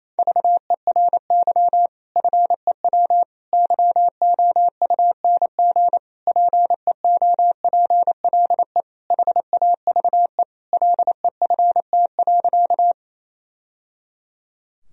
Morse code, heard as a keyed tone, about 28 wpm, 700 Hz. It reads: VERY FEW YOUNG PEOPLE HAVE LEFT.
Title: Sentences from Top 1000 Words - Review Part 1 - 28wpm